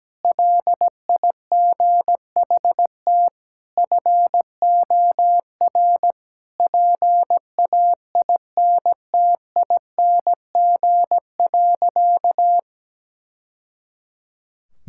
Morse code, heard as a keyed tone, about 17 wpm, 700 Hz. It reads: LIGHT FOR PAINTING.
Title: Sentences from Top 1000 Words - Review Part 6 - 17wpm